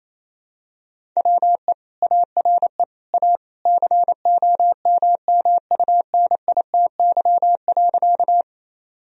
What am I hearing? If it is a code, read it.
Morse code, heard as a keyed tone, about 28 wpm, 700 Hz. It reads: WE ARE A COMMUNITY.
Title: Sentences from Top 1000 Words - Review Part 4 - 28wpm